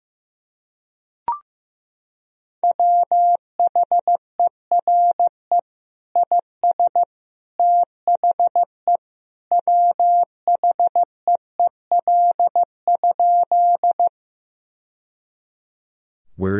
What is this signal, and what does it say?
Morse code, heard as a keyed tone, about 15 wpm, 700 Hz. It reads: WHERE IS THE WHEEL?
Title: Sentences from Top 1000 Words - Review Part 3 - 15wpm